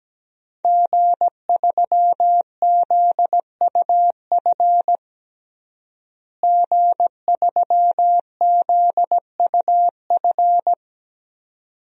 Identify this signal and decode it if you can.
Morse code, heard as a keyed tone, about 17 wpm, 700 Hz. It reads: G3ZUF G3ZUF
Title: DXCC-100 Call Signs - 5 Characters - Repeated 2x - 17wpm